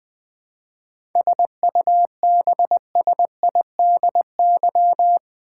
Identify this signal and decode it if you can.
Morse code, heard as a keyed tone, about 20 wpm, 700 Hz. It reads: SUBSIDY
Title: 7 Letter Words - First 3 Letters Given - 20wpm